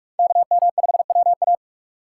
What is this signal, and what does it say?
Morse code, heard as a keyed tone, about 43 wpm, 700 Hz. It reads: KM5WA
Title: USA Call Signs - 5 Characters - 6x Character Spacing - 45wpm